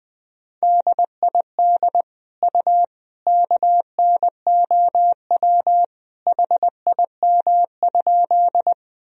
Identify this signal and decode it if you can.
Morse code, heard as a keyed tone, about 20 wpm, 700 Hz. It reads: DID U KNOW HIM?
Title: Sentences from Top 100 Words and Abbreviations 20wpm